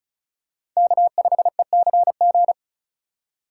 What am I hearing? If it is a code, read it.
Morse code, heard as a keyed tone, about 36 wpm, 700 Hz. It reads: K5ECG
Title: USA Call Signs - 5 Characters - 4x Character Spacing - 35wpm